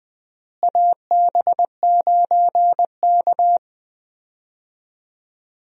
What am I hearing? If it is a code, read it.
Morse code, heard as a keyed tone, about 21 wpm, 700 Hz. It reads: AB9K
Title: USA Call Signs - 4 Characters - Repeated 2x - 20wpm